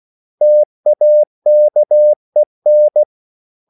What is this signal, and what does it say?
Morse code, heard as a keyed tone, about 16 wpm, 600 Hz. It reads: TAKEN
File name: CW / Practice / 500MostCommonEnglishWords16